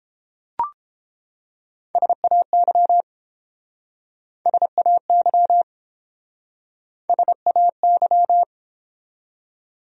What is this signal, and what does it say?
Morse code, heard as a keyed tone, about 31 wpm, 700 Hz. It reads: SAY SAY SAY